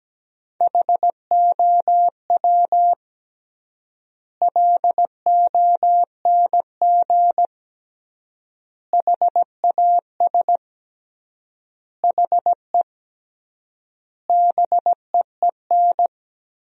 Morse code, heard as a keyed tone, about 17 wpm, 700 Hz. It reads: HOW LONG HAS HE BEEN
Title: 3x Word-Spacing - Sentences from Top 100 Words - 17wpm